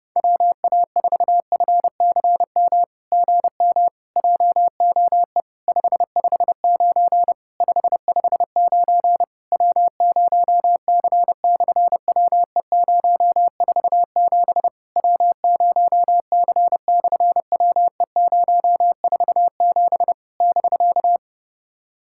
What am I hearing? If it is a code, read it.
Morse code, heard as a keyed tone, about 30 wpm, 700 Hz. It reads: WA4FCM GM JOE 559 559 W0C/WE047 W0C/WE047 <BK>
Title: SOTA QSO Practice 30wpm